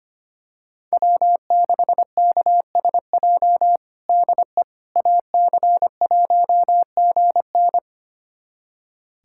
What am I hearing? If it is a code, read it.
Morse code, heard as a keyed tone, about 25 wpm, 700 Hz. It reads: W6KSJ DE AC1GN
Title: USA Callsign DE Callsign - 5 Characters - 25wpm